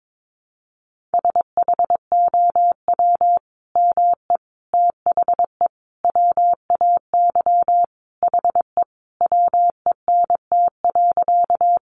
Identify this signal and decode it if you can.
Morse code, heard as a keyed tone, about 22 wpm, 700 Hz. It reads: SHOW ME THE WAY HE WENT.
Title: Sentences from Top 200 Words 22wpm